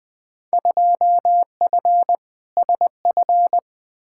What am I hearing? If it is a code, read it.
Morse code, heard as a keyed tone, about 20 wpm, 700 Hz. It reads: 2F SF
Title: Field Day Exchanges - 20wpm